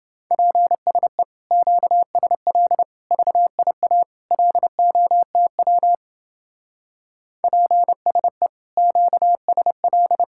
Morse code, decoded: PSE QSL VIA LOTW PSE QSL

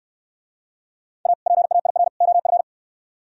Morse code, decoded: A PAIR OF

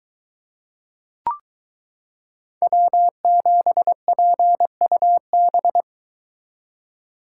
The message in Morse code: W7PUB

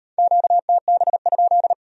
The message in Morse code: QTB?

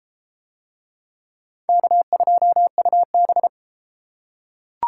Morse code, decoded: K2UB

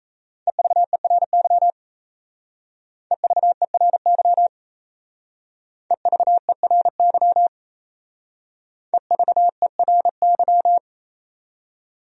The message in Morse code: EVERY EVERY EVERY EVERY

